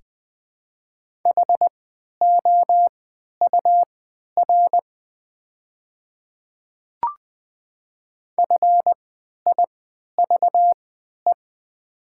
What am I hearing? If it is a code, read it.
Morse code, heard as a keyed tone, about 20 wpm, spaced as for 11 wpm, 700 Hz. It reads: HOUR FIVE
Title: 3x Character Spacing - Top 500 Words - 20wpm